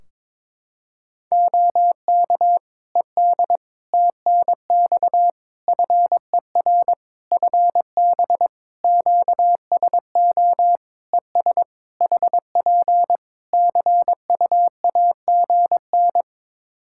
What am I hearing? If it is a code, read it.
Morse code, heard as a keyed tone, about 22 wpm, 700 Hz. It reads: OK ED TNX FER FB QSO ES HP CUAGN
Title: Practice QSO - Exchange 4 - Ending - 22wpm